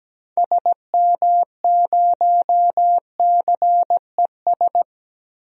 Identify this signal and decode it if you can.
Morse code, heard as a keyed tone, about 17 wpm, 700 Hz. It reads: SM0CES
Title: DXCC-100 Call Signs - 6 Characters - Repeated 2x - 17wpm